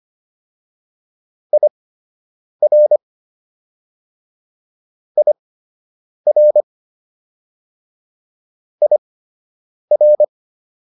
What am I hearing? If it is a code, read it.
Morse code, heard as a keyed tone, about 25 wpm, spaced as for 8 wpm, 600 Hz. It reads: IR IR IR